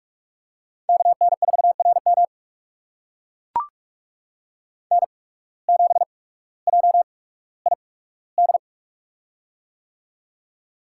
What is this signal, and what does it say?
Morse code, heard as a keyed tone, about 44 wpm, spaced as for 19 wpm, 700 Hz. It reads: KN4RM N7JID
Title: USA Call Signs - 5 Characters - 8x Character Spacing - 45wpm